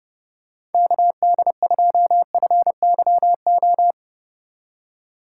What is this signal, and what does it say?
Morse code, heard as a keyed tone, about 30 wpm, 700 Hz. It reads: KD2FYO
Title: USA Call Signs - 6 Characters - 2x Character Spacing - 30wpm